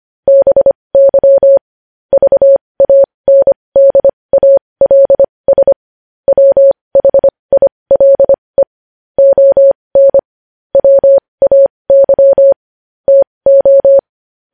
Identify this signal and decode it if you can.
Morse code, heard as a keyed tone, about 25 wpm, spaced as for 20 wpm, 550 Hz. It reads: BY VANDALS WHILE ON WAY TO
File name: podcast / News headlines-1x-25x20